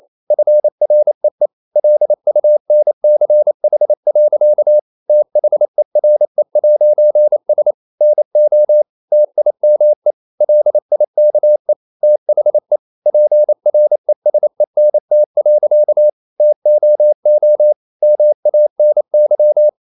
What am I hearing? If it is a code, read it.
Morse code, heard as a keyed tone, about 28 wpm, 600 Hz. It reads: FREE LUNCH. THERE'S NO TIME LIKE THE PRESENT. TOO MANY
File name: CW / Practice / EnglishSayings28